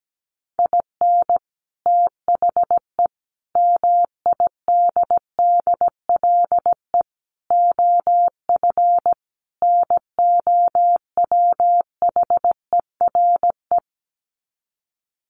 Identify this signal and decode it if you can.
Morse code, heard as a keyed tone, about 17 wpm, 700 Hz. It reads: IN THE MIDDLE OF NOWHERE
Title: Sets of 5 Words Encore 17wpm